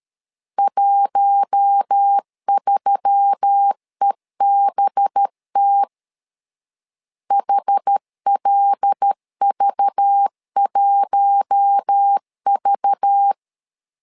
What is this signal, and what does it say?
Morse code, heard as a keyed tone, about 13 wpm, 800 Hz. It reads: 13EBT HLV1V
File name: LEZIONI / 5 LBVC / 5 LBV 12wpm 1855